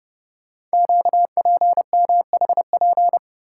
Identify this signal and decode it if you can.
Morse code, heard as a keyed tone, about 29 wpm, 700 Hz. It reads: QPMHP
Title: Code Groups - 5 Letters - 30wpm